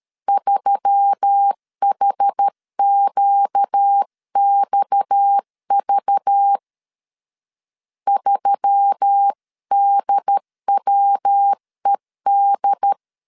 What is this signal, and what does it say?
Morse code, heard as a keyed tone, about 13 wpm, 800 Hz. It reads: 3HQXV 3DWED
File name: CW / DET / 1 6 1500C S C 12wpm 3533